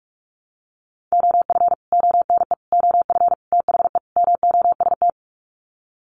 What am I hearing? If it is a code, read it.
Morse code, heard as a keyed tone, about 44 wpm, 700 Hz. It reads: OF ONE OF THE MOST
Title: Sets of 5 Words Encore2 45wpm